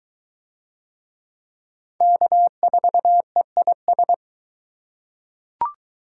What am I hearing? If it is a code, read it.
Morse code, heard as a keyed tone, about 21 wpm, 700 Hz. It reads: K4EIS